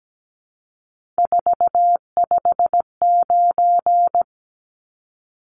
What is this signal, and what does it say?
Morse code, heard as a keyed tone, about 17 wpm, 700 Hz. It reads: 459
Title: Numbers - 3-Digit 17wpm